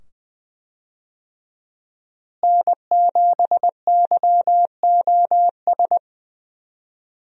Morse code, heard as a keyed tone, about 20 wpm, 700 Hz. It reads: N7YOS
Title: USA Call Signs - 5 Characters - Repeated 2x - 20wpm